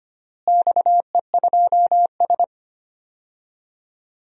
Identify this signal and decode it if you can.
Morse code, heard as a keyed tone, about 25 wpm, 700 Hz. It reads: XE2S
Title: DXCC-100 Call Signs - 4 Characters - Repeated 3x - 25wpm